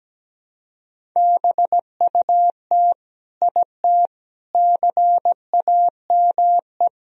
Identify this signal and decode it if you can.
Morse code, heard as a keyed tone, about 17 wpm, 700 Hz. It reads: BUT IT CAME
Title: Sentences from Top 1000 Words - Review Part 2 - 17wpm